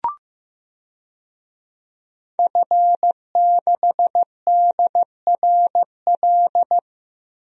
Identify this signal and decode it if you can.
Morse code, heard as a keyed tone, about 15 wpm, 700 Hz. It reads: F6DRL
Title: DXCC-100 Call Signs - 5 Characters - Repeated 3x - 15wpm